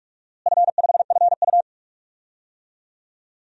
Morse code, heard as a keyed tone, about 46 wpm, 700 Hz. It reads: F5FU
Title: DXCC-100 Call Signs - 4 Characters - Repeated 2x - 45wpm